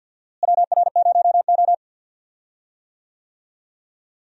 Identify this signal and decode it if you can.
Morse code, heard as a keyed tone, about 48 wpm, 700 Hz. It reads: WR0O
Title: USA Call Signs - 4 Characters - Repeated 3x - 50wpm